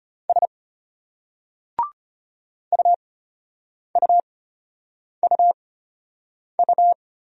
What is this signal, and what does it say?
Morse code, heard as a keyed tone, about 34 wpm, spaced as for 8 wpm, 700 Hz. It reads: S UUUU